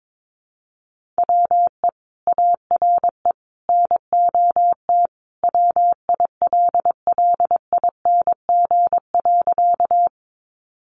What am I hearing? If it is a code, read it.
Morse code, heard as a keyed tone, about 22 wpm, 700 Hz. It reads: WE ARE NOT WILLING.
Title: Sentences from Top 1000 Words 22wpm